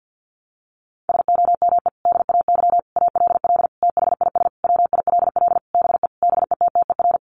Morse code, heard as a keyed tone, about 50 wpm, 700 Hz. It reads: SOME DAY ALL THIS WILL BE BETTER
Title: Sentences from Top 1300 Words 50wpm